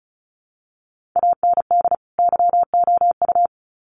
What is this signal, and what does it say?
Morse code, heard as a keyed tone, about 34 wpm, 700 Hz. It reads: AND YOU
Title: Sets of 2 Words 35wpm